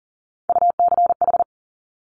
Morse code, heard as a keyed tone, about 38 wpm, 700 Hz. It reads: UCH
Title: Top Three Letter Combinations in Words 40wpm